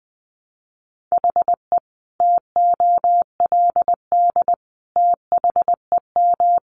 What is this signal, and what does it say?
Morse code, heard as a keyed tone, about 20 wpm, 700 Hz. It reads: HE TOLD THEM
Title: Sentences from Top 1200 Words 20wpm